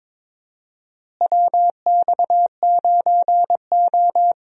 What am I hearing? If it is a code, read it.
Morse code, heard as a keyed tone, about 21 wpm, 700 Hz. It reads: WX9O
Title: USA Call Signs - 4 Characters - Repeated 2x - 22wpm